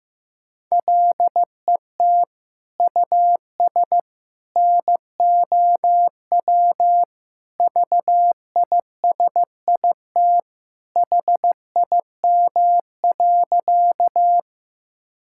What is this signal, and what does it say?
Morse code, heard as a keyed tone, about 15 wpm, 700 Hz. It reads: LET US NOW VISIT HIM.
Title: Sentences from Top 1000 Words - Review Part 6 - 15wpm